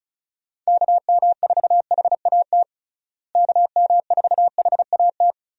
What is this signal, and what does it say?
Morse code, heard as a keyed tone, about 36 wpm, 700 Hz. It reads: KM4HAT KM4HAT
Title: USA Call Signs - 6 Characters - Repeated 2x - 35wpm